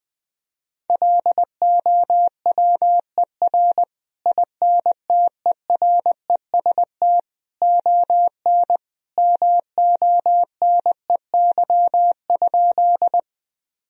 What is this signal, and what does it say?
Morse code, heard as a keyed tone, about 20 wpm, 700 Hz. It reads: LOWER INTEREST ON MONEY?
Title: Sentences from Top 1000 Words - Review Part 4 - 20wpm